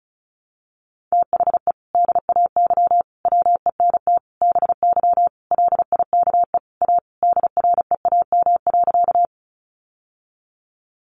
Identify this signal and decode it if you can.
Morse code, heard as a keyed tone, about 34 wpm, 700 Hz. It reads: THE DAY WENT BY LIKE A DREAM.
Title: Sentences from Top 1500 Words 35wpm